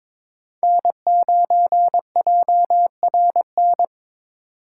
Morse code, heard as a keyed tone, about 22 wpm, 700 Hz. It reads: N9JRN
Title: USA Call Signs - 5 Characters - 4x Character Spacing - 22wpm